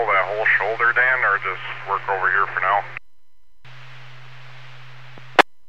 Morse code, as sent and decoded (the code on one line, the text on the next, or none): none